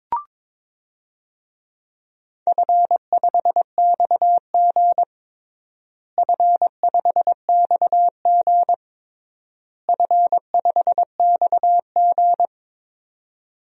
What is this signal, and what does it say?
Morse code, heard as a keyed tone, about 22 wpm, 700 Hz. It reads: F5XG F5XG F5XG
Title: DXCC-100 Call Signs - 4 Characters - Repeated 3x - 22wpm